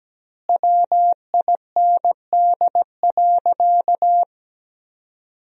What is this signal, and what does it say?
Morse code, heard as a keyed tone, about 17 wpm, 700 Hz. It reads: WIND.
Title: 2x Word-Spacing - Sentences from Top 400 Words - 17wpm